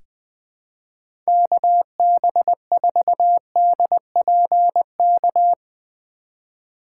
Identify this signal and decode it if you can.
Morse code, heard as a keyed tone, about 21 wpm, 700 Hz. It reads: KB4DPK
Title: USA Call Signs - 6 Characters - 3x Character Spacing - 20wpm